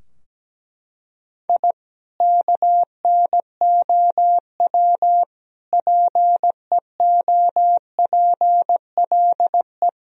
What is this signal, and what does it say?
Morse code, heard as a keyed tone, about 17 wpm, 700 Hz. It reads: I KNOW PEOPLE
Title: Sentences from Top 1000 Words - Review Part 4 - 17wpm